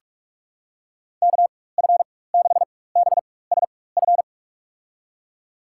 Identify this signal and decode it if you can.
Morse code, heard as a keyed tone, about 46 wpm, spaced as for 21 wpm, 700 Hz. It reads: KF6BSF
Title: USA Call Signs - 6 Characters - 4x Character Spacing - 45wpm